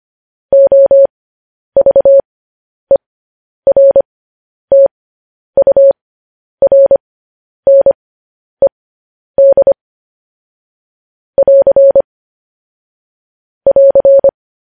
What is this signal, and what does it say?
Morse code, heard as a keyed tone, about 25 wpm, spaced as for 10 wpm, 550 Hz. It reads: OVERTURNED <AR> <AR>
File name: podcast / News headlines-1x-25x10